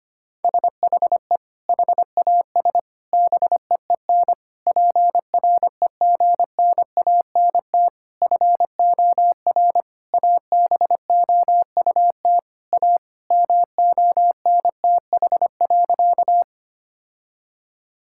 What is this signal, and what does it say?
Morse code, heard as a keyed tone, about 25 wpm, 700 Hz. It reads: SHE HAS BEEN PREGNANT FOR ABOUT A MONTH.
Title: Sentences from Top 1000 Words - Review Part 6 - 25wpm